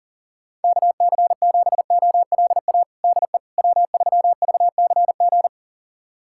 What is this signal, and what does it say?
Morse code, heard as a keyed tone, about 37 wpm, 700 Hz. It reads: KC7OLA DE W3VCG
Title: USA Callsign DE Callsign 40wpm